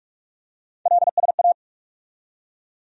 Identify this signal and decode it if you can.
Morse code, heard as a keyed tone, about 46 wpm, 700 Hz. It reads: LSA
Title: USA Call Sign Postfixes - 3 Letters 45wpm